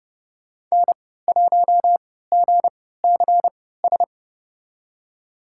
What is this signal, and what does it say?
Morse code, heard as a keyed tone, about 32 wpm, spaced as for 17 wpm, 700 Hz. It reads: N1GCS
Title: USA Call Signs - 5 Characters - 3x Character Spacing - 30wpm